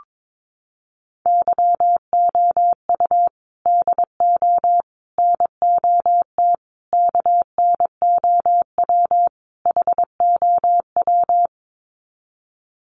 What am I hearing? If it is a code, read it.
Morse code, heard as a keyed tone, about 22 wpm, 700 Hz. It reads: YOU DO NOT KNOW HOW
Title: Sets of 5 Words Encore 22wpm